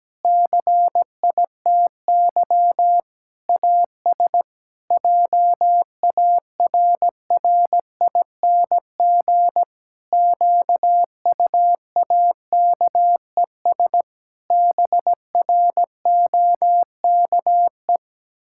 Morse code, heard as a keyed tone, about 17 wpm, 700 Hz. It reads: CITY AS JARRING QUAKES BROKE
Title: Pangram Challenge v2 - 17wpm